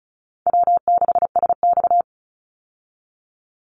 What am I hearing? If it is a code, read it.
Morse code, heard as a keyed tone, about 34 wpm, 700 Hz. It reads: W6SX
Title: Easy Call Signs Repeated 2x - 35wpm